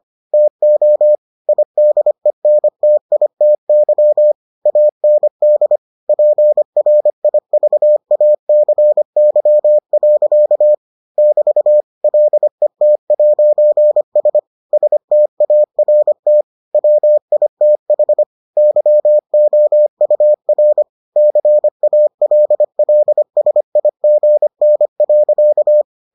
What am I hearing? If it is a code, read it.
Morse code, heard as a keyed tone, about 25 wpm, 600 Hz. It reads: TO IDENTITY AND PRIVACY. <BT> LET'S START WITH YOUR CALLSIGN.